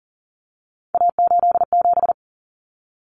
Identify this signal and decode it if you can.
Morse code, heard as a keyed tone, about 40 wpm, 700 Hz. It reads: A87
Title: Common Cut Numbers - 3-Digit 40wpm